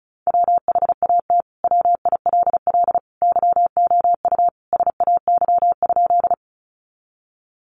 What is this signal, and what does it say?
Morse code, heard as a keyed tone, about 34 wpm, 700 Hz. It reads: WHAT WILL YOU SAY?
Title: Sentences from Top 200 Words 35wpm